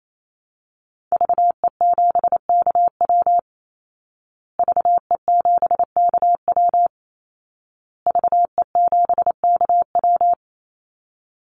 Morse code, heard as a keyed tone, about 28 wpm, 700 Hz. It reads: VE7KW VE7KW VE7KW
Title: Call Signs Repeated 2x - 28wpm